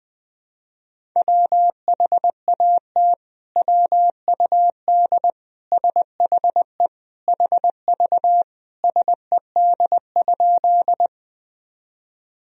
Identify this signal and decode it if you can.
Morse code, heard as a keyed tone, about 21 wpm, 700 Hz. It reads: WHAT WUD SHE HV SED?
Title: Sentences from Top 100 Words and Abbreviations 20wpm